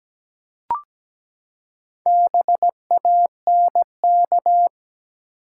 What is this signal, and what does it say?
Morse code, heard as a keyed tone, about 17 wpm, 700 Hz. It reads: BANK
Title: Four Letter Words - 17wpm